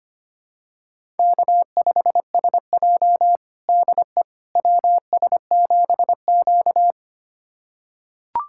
Morse code, heard as a keyed tone, about 25 wpm, 700 Hz. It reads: K5SJ DE WS7Q
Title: USA Callsign DE Callsign - 4 Characters - 25wpm